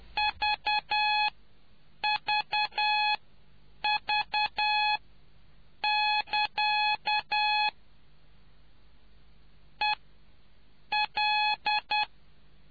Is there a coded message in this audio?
Yes: Morse code from 0.2 to 12.0 s, heard as a keyed tone, about 10 wpm, spaced as for 7 wpm, 840 Hz. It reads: VVV<KA> EL